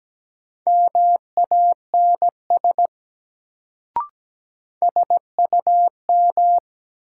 Morse code, heard as a keyed tone, about 17 wpm, 700 Hz. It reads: MANS SUM
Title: Sentences from Top 1000 Words - Review Part 6 - New Word Drill - 17wpm